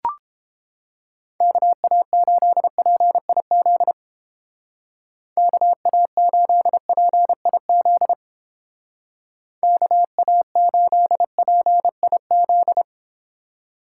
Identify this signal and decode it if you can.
Morse code, heard as a keyed tone, about 29 wpm, 700 Hz. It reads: KA8PIZ KA8PIZ KA8PIZ